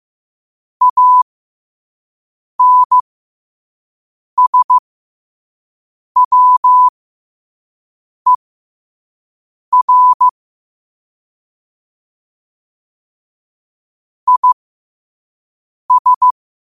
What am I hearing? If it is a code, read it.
Morse code, heard as a keyed tone, about 15 wpm, spaced as for 5 wpm, 1000 Hz. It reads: ANSWER IS
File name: qotdfiles / 5wpm